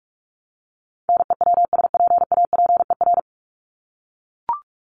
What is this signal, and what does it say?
Morse code, heard as a keyed tone, about 44 wpm, 700 Hz. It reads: NEWSPAPER
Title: Top 751-1000 Words 45wpm